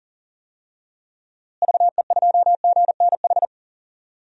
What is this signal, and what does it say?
Morse code, heard as a keyed tone, about 40 wpm, 700 Hz. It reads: VE2GNH
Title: DXCC-100 Call Signs - 6 Characters - Repeated 3x - 40wpm